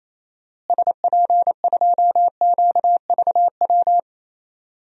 Morse code, heard as a keyed tone, about 28 wpm, 700 Hz. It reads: SP2QVW
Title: DXCC-100 Call Signs - 6 Characters - Repeated 3x - 28wpm